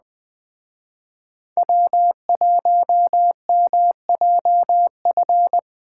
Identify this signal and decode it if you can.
Morse code, heard as a keyed tone, about 20 wpm, 700 Hz. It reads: W1MJF
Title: Speed Racing - USA 1x3 Call Signs 17wpm